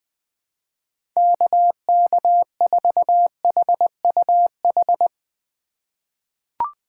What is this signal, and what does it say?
Morse code, heard as a keyed tone, about 20 wpm, 700 Hz. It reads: KK4HUH
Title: USA Call Signs - 6 Characters - 4x Character Spacing - 20wpm